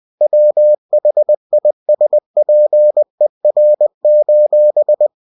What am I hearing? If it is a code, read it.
Morse code, heard as a keyed tone, about 20 wpm, 600 Hz. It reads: WHISPER:
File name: cw-books / winnie-the-pooh / winnie-20wpm-page-0005